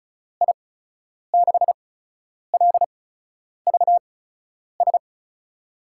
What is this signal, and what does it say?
Morse code, heard as a keyed tone, about 36 wpm, spaced as for 10 wpm, 700 Hz. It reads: I6LVS
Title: USA Call Signs - 8x Character Spacing - 35wpm